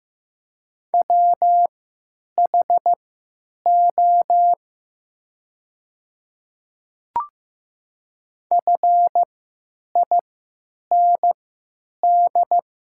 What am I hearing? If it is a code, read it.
Morse code, heard as a keyed tone, about 15 wpm, spaced as for 9 wpm, 700 Hz. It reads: WHO FIND
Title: 3x Character Spacing - Top 100 Words - 15wpm